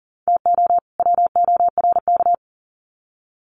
Morse code, heard as a keyed tone, about 40 wpm, 700 Hz. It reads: TO WORK
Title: Sets of 2 Words 40wpm